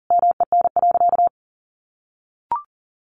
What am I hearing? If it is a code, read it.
Morse code, heard as a keyed tone, about 40 wpm, 700 Hz. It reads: MEN.
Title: A Sound Of Thunder 40wpm